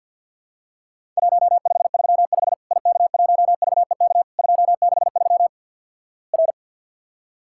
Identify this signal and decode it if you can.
Morse code, heard as a keyed tone, about 48 wpm, 700 Hz. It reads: 1535 IK1VEK 263 R
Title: Worked All Europe DX Contest - QTC Exchange - 50wpm